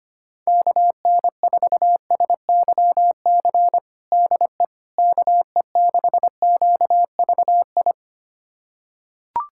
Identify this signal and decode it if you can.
Morse code, heard as a keyed tone, about 25 wpm, 700 Hz. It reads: KN4SYC DE KE6QVI
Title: USA Callsign DE Callsign - 6 Characters - 25wpm